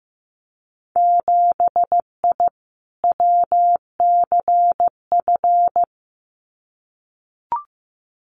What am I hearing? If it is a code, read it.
Morse code, heard as a keyed tone, about 15 wpm, 700 Hz. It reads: 7I WCF
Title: Winter Field Day Exchanges Encore 15wpm